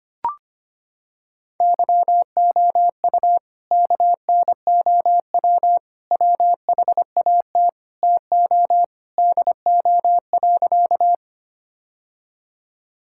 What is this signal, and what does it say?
Morse code, heard as a keyed tone, about 25 wpm, 700 Hz. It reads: YOU KNOW WHAT TO DO.
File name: practice / 25wpm / Sentences with 3-Word Phrases - 25wpm